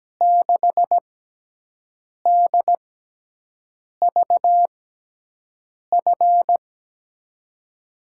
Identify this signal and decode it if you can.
Morse code, heard as a keyed tone, about 17 wpm, spaced as for 6 wpm, 700 Hz. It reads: 6DVF
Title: USA Call Signs - 6 Characters - 6x Character Spacing - 17wpm